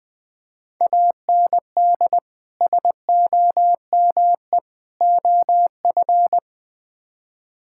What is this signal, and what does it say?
Morse code, heard as a keyed tone, about 20 wpm, 700 Hz. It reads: AND SOME OF
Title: Build A Phrase - 4-Word Phrases - 20wpm